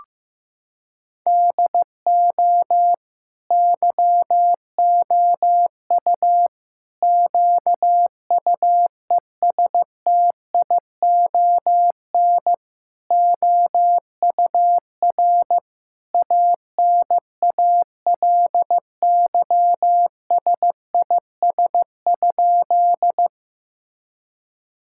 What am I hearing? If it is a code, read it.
Morse code, heard as a keyed tone, about 15 wpm, 700 Hz. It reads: DO YOU QUESTION OUR ANALYSIS?
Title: Sentences from Top 1000 Words - Review Part 5 - 15wpm